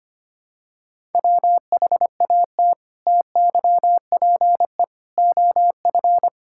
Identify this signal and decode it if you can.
Morse code, heard as a keyed tone, about 25 wpm, 700 Hz. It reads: WHAT TYPE OF
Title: Sentences from Top 1000 Words - Review Part 6 - 25wpm